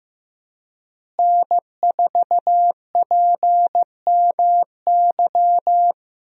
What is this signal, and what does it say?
Morse code, heard as a keyed tone, about 15 wpm, 700 Hz. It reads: N4PMY
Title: USA Call Signs - 5 Characters - Repeated 2x - 15wpm